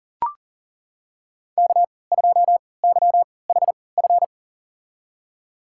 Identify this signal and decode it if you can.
Morse code, heard as a keyed tone, about 40 wpm, spaced as for 23 wpm, 700 Hz. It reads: K2YHF
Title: USA Call Signs - 3x Character Spacing - 40wpm